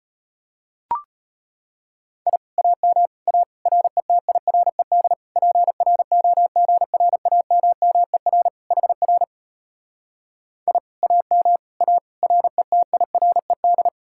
Morse code, heard as a keyed tone, about 33 wpm, 700 Hz. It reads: I AM A RETIRED PROGRAMMER HR I AM A RETIRED